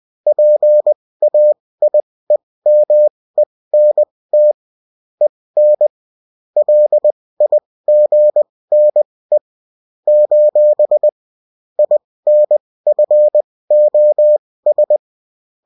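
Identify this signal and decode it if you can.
Morse code, heard as a keyed tone, about 20 wpm, spaced as for 16 wpm, 600 Hz. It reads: PAIEMENT EN LIGNE : INFOS
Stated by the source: Bulletin in text and CW form.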